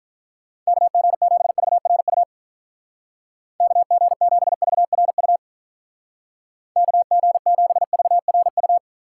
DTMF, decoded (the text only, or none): none